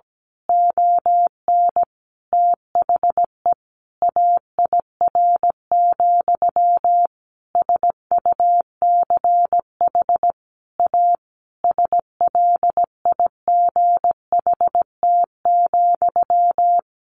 Morse code, heard as a keyed tone, about 17 wpm, 700 Hz. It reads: ON THE AIR, SUCH A SLIGHT,
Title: A Sound Of Thunder 17wpm